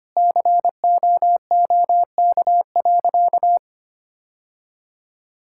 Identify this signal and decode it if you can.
Morse code, heard as a keyed tone, about 25 wpm, 700 Hz. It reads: COOK.
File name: practice / 25wpm / Sentences with 2-Word Phrases - 25wpm